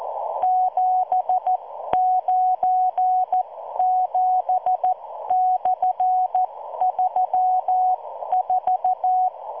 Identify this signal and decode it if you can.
Morse code, heard as a keyed tone, about 14 wpm, spaced as for 12 wpm, 740 Hz. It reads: 797/34